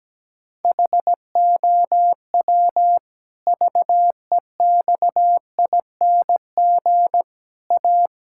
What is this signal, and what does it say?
Morse code, heard as a keyed tone, about 17 wpm, 700 Hz. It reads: HOW VEXING A
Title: Pangram Challenge v2 - 17wpm